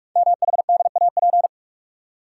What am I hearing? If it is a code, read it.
Morse code, heard as a keyed tone, about 44 wpm, 690 Hz. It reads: MHDAP
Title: Code Groups - 5 Letters - 45wpm